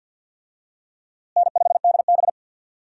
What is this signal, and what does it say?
Morse code, heard as a keyed tone, about 47 wpm, 700 Hz. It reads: N5DB
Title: USA Call Signs - 4 Characters - Repeated 3x - 50wpm